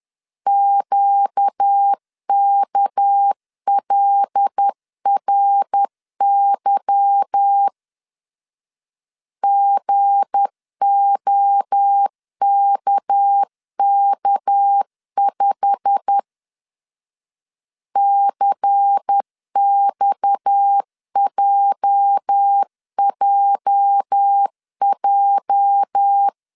Morse code, heard as a keyed tone, about 10 wpm, 800 Hz. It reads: QKLRY GOKK5 CXJJJ
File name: CW / DET / 1 9 1500C S B 10wpm 4439